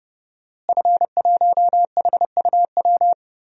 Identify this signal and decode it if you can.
Morse code, heard as a keyed tone, about 30 wpm, 700 Hz. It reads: F1HUW
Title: DXCC-100 Call Signs - 5 Characters - Repeated 3x - 30wpm